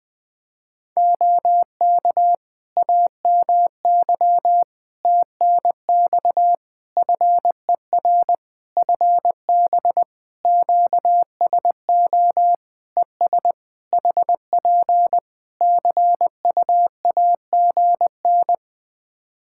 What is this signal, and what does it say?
Morse code, heard as a keyed tone, about 20 wpm, 700 Hz. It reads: OK AMY TNX FER FB QSO ES HP CUAGN
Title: Practice QSO - Exchange 4 - Ending - 20wpm